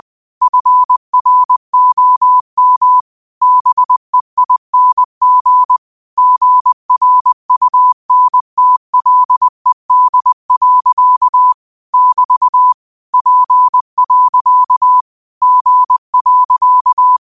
Morse code, heard as a keyed tone, about 20 wpm, 1000 Hz. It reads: FROM BEING GRUNTLED. <BT> P. G.
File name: qotdfiles / 20wpm